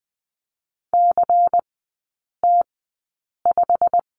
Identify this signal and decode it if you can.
Morse code, heard as a keyed tone, about 20 wpm, spaced as for 8 wpm, 700 Hz. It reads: CT5
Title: 093 Lesson 16e - Letter c - Three Characters 20wpm